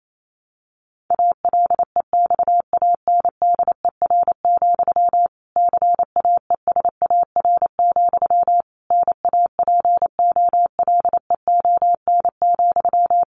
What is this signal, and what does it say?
Morse code, heard as a keyed tone, about 28 wpm, 700 Hz. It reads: ALEXANDER, CAESAR, NAPOLEON,
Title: A Sound Of Thunder 28wpm